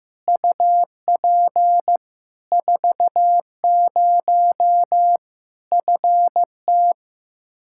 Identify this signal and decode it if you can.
Morse code, heard as a keyed tone, about 15 wpm, 700 Hz. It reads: UP 40 FT
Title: K1USN SST QSO Practice 15wpm